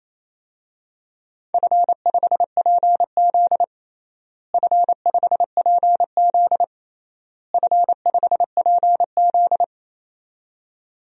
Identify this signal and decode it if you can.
Morse code, heard as a keyed tone, about 28 wpm, 700 Hz. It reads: F5PZ F5PZ F5PZ
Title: DXCC-100 Call Signs - 4 Characters - Repeated 3x - 28wpm